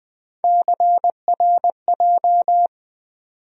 Morse code, heard as a keyed tone, about 20 wpm, 700 Hz. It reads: CRJ
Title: USA Call Sign Postfixes - 3 Letters 20wpm